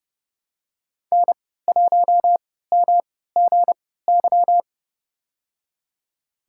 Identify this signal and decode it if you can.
Morse code, heard as a keyed tone, about 29 wpm, spaced as for 17 wpm, 700 Hz. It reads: N1MGY
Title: USA Call Signs - 5 Characters - 3x Character Spacing - 30wpm